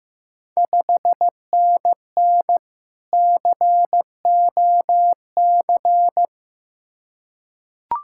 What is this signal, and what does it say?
Morse code, heard as a keyed tone, about 15 wpm, 700 Hz. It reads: 5NN COC
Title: Ontario QSO Party Exchange 15wpm